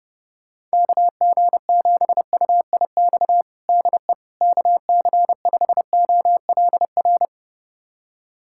Morse code, heard as a keyed tone, about 28 wpm, 700 Hz. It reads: KG7UIX DE KC5OLR
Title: USA Callsign DE Callsign - 6 Characters - 30wpm